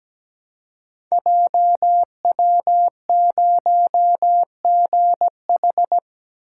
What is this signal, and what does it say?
Morse code, heard as a keyed tone, about 17 wpm, 700 Hz. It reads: JW0GH
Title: Code Groups - 5 Letter:Numbers - 17wpm